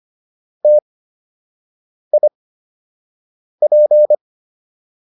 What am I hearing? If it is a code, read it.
Morse code, heard as a keyed tone, about 25 wpm, spaced as for 6 wpm, 600 Hz. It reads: TIP